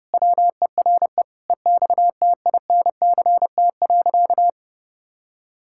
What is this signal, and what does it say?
Morse code, heard as a keyed tone, about 30 wpm, 700 Hz. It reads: WERE EXTINCT.
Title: Pangram Challenge v2 - 30wpm